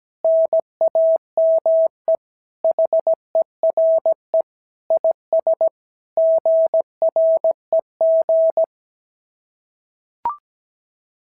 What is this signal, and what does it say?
Morse code, heard as a keyed tone, about 16 wpm, 650 Hz. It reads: NAME HERE IS GREG
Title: SOTA QSO Practice 17wpm